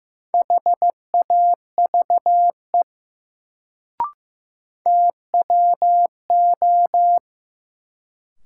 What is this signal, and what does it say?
Morse code, heard as a keyed tone, about 15 wpm, 700 Hz. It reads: HAVE TWO
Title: Sentences from Top 100 Words - New Word Drill - 15wpm